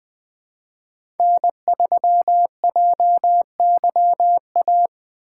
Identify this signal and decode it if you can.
Morse code, heard as a keyed tone, about 20 wpm, 700 Hz. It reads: N3JYA
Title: USA Call Signs - 2x Character Spacing - 20wpm